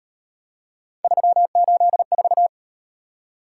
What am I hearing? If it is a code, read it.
Morse code, heard as a keyed tone, about 38 wpm, 700 Hz. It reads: 384